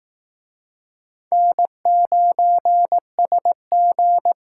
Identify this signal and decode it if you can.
Morse code, heard as a keyed tone, about 20 wpm, 700 Hz. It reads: N9SG